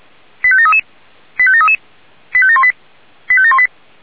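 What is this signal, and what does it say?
Five-tone selective call (ZVEI), heard as quick runs of five tones, tones of about 70 ms. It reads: 85733 85733 85718 85718
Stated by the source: ZVEI five-tone selective call received on SDR.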